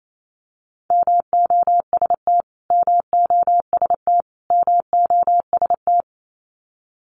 Morse code, heard as a keyed tone, about 28 wpm, 700 Hz. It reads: MOST MOST MOST
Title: Top 300 Words Repeated 28wpm